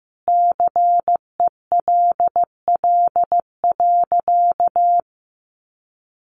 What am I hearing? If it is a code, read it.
Morse code, heard as a keyed tone, about 15 wpm, 700 Hz. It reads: CELL.
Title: Sentences from Top 1100 Words 15wpm